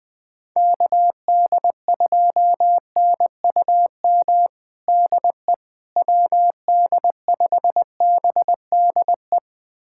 Morse code, heard as a keyed tone, about 20 wpm, 700 Hz. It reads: KD2NUM DE WD5BDE
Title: USA Callsign DE Callsign 20wpm